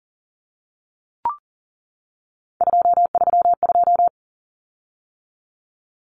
Morse code, heard as a keyed tone, about 40 wpm, 700 Hz. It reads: 232
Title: Common Cut Numbers - 3-Digit 40wpm